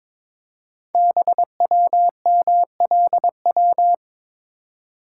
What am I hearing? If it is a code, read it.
Morse code, heard as a keyed tone, about 22 wpm, 700 Hz. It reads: BWMLW
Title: Code Groups - 5 Letters - 22wpm